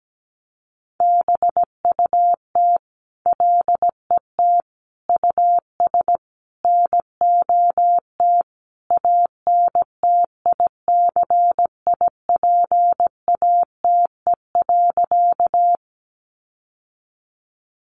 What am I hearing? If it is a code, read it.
Morse code, heard as a keyed tone, about 17 wpm, 700 Hz. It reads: BUT LET US NOT ANTICIPATE.
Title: Sentences from Top 1500 Words 17wpm